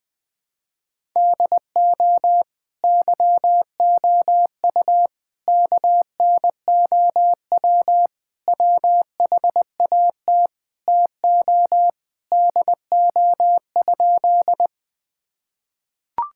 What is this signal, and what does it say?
Morse code, heard as a keyed tone, about 20 wpm, 700 Hz. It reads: DO YOU KNOW WHAT TO DO?
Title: Sentences with 2-Word Phrases - 20wpm